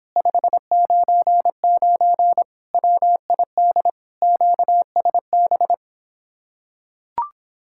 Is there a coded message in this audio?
Yes: Morse code from 0.2 to 5.7 s, heard as a keyed tone, about 26 wpm, 700 Hz. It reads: 599 WID QSB